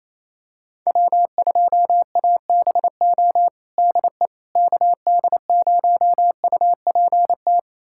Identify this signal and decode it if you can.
Morse code, heard as a keyed tone, about 28 wpm, 700 Hz. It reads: W2ABO DE KD0UPT
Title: USA Callsign DE Callsign 28wpm